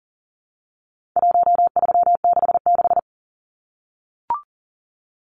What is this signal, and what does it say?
Morse code, heard as a keyed tone, about 40 wpm, 700 Hz. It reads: 1366
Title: Numbers - Contest Style - 4-Digit 40wpm